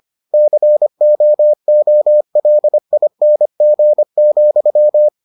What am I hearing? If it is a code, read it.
Morse code, heard as a keyed tone, about 25 wpm, 600 Hz. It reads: COOLING,